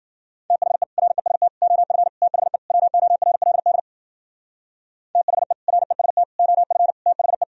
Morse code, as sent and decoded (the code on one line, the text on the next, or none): - .... / .-. ... - / --- ..-. / - .... / .-- --- .-. .-.. -.. / - .... / .-. ... - / --- ..-. / - .... .
THE REST OF THE WORLD THE REST OF THE